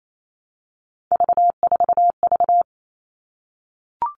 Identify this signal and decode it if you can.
Morse code, heard as a keyed tone, about 28 wpm, 700 Hz. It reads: V4V
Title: All Cut Numbers - 3-Digit 28wpm